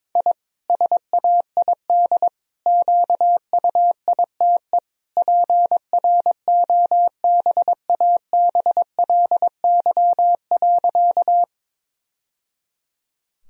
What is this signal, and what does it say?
Morse code, heard as a keyed tone, about 22 wpm, 700 Hz. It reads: I SAID QUITE PROBABLY.
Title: Sentences from Top 1000 Words - Review Part 4 - 22wpm